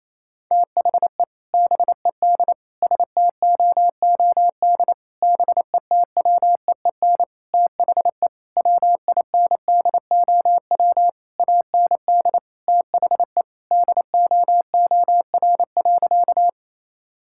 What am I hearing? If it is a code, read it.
Morse code, heard as a keyed tone, about 28 wpm, 700 Hz. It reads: THE BED STOOD BETWEEN THE WINDOW AND THE DOOR.
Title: Sentences from Top 1600 Words 28wpm